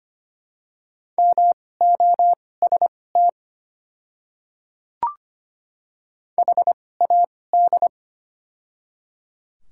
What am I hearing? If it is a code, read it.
Morse code, heard as a keyed tone, about 25 wpm, spaced as for 18 wpm, 700 Hz. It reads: MOST HAD
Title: 2x Character Spacing - Top 100 Words - 25wpm